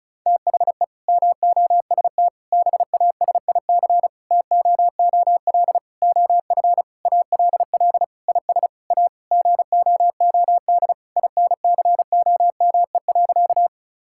Morse code, heard as a keyed tone, about 36 wpm, 700 Hz. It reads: THE MOST BASIC TOOL OF ALL IS A GOOD INCOME.
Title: Sentences from Top 1000 Words - Review Part 6 - 35wpm